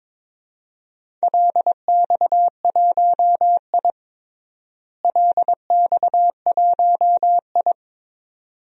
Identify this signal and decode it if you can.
Morse code, heard as a keyed tone, about 21 wpm, 700 Hz. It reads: LX1I LX1I
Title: DXCC-100 Call Signs - 4 Characters - Repeated 2x - 22wpm